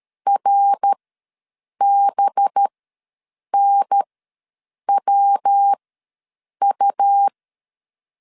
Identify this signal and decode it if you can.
Morse code, heard as a keyed tone, about 13 wpm, spaced as for 7 wpm, 800 Hz. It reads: RBNWU
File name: 5 LB ws 12wpm